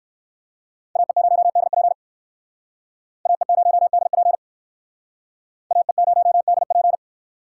Morse code, heard as a keyed tone, about 57 wpm, 700 Hz. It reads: AE0DP AE0DP AE0DP